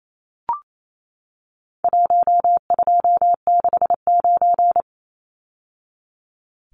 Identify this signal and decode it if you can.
Morse code, heard as a keyed tone, about 28 wpm, 700 Hz. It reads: 1269
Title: Numbers - Contest Style - 4-Digit 28wpm